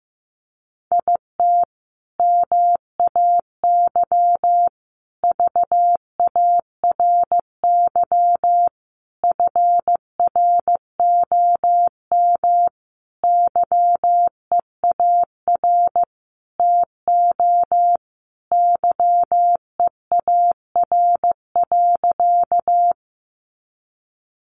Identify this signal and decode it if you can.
Morse code, heard as a keyed tone, about 15 wpm, 700 Hz. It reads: IT MAY VARY FROM YEAR TO YEAR.
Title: Sentences from Top 1200 Words 15wpm